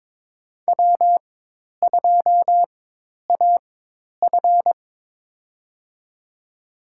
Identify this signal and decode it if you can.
Morse code, heard as a keyed tone, about 22 wpm, spaced as for 10 wpm, 700 Hz. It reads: W2AF
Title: USA Call Signs - 4 Characters - 4x Character Spacing - 22wpm